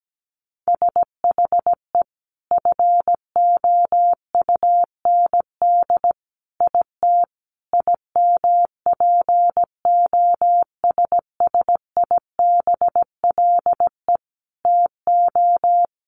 Morse code, heard as a keyed tone, about 17 wpm, 700 Hz. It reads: SHE FOUND IT IMPOSSIBLE TO
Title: Sentences from Top 1100 Words 17wpm